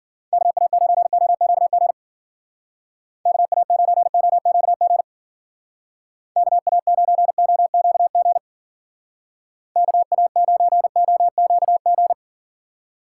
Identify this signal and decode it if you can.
Morse code, heard as a keyed tone, about 51 wpm, 700 Hz. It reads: KA9OQG KA9OQG KA9OQG KA9OQG